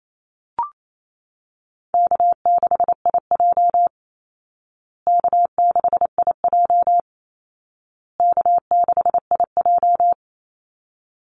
Morse code, heard as a keyed tone, about 28 wpm, 700 Hz. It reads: K6IJ K6IJ K6IJ
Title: Easy Call Signs Repeated 3x - 28wpm